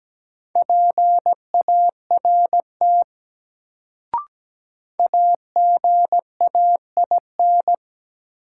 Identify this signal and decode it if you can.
Morse code, heard as a keyed tone, about 17 wpm, 700 Hz. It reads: PART AGAIN
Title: Sentences from Top 200 Words - New Word Drill - 17wpm